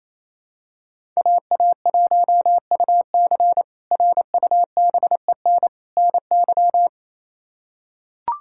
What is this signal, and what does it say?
Morse code, heard as a keyed tone, about 28 wpm, 700 Hz. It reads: AA1UC RUBEN NY
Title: K1USN SST QSO Practice - 28wpm